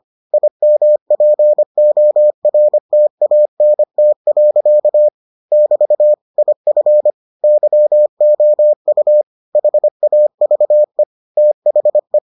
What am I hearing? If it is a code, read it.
Morse code, heard as a keyed tone, about 25 wpm, 600 Hz. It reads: IMPORTANT. <BT> IF YOU HAVE THE